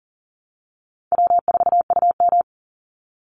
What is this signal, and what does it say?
Morse code, heard as a keyed tone, about 41 wpm, 700 Hz. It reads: W4UM
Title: Easy Call Signs - 40wpm